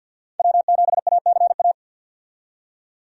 Morse code, heard as a keyed tone, about 50 wpm, 700 Hz. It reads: W7ACA
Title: USA Call Signs - 5 Characters - 50wpm